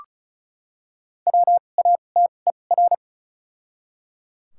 Morse code, heard as a keyed tone, about 34 wpm, spaced as for 25 wpm, 700 Hz. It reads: WATER
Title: 2x Character Spacing - Top 300 Words - 35wpm